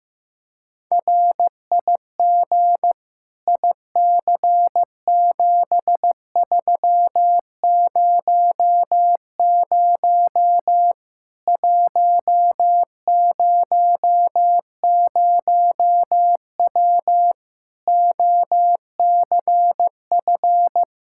Morse code, heard as a keyed tone, about 15 wpm, 700 Hz. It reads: RIG IC7300 100W OCF
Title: Practice QSO - Exchange 2 - Info - 15wpm